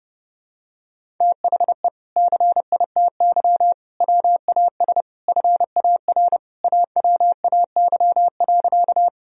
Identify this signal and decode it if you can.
Morse code, heard as a keyed tone, about 30 wpm, 700 Hz. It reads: THE CITY WAS FAR AWAY.
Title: Sentences from Top 1000 Words - Review Part 1 - 30wpm